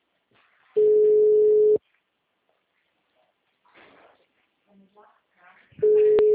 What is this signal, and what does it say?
Ringback tone (Europe, 425 Hz), with speech.
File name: out- -2007- - - .40